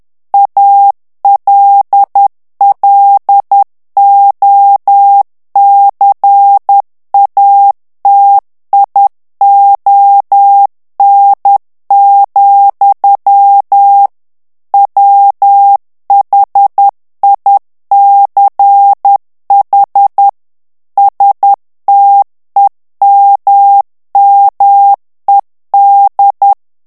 Morse code, heard as a keyed tone, about 10 wpm, 800 Hz. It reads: ALLOCATION, WHICH STEMMED
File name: ANGLAIS texte clair 3 vitesse 10 mots minute